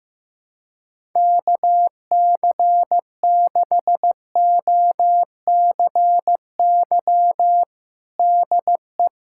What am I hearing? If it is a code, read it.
Morse code, heard as a keyed tone, about 15 wpm, 700 Hz. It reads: KC6OCY DE
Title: USA Callsign DE Callsign - 6 Characters - 15wpm